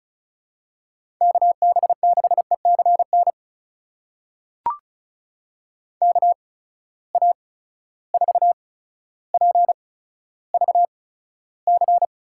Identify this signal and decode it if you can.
Morse code, heard as a keyed tone, about 34 wpm, spaced as for 15 wpm, 700 Hz. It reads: KB6ECN KA4PVC
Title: USA Call Signs - 6 Characters - 8x Character Spacing - 35wpm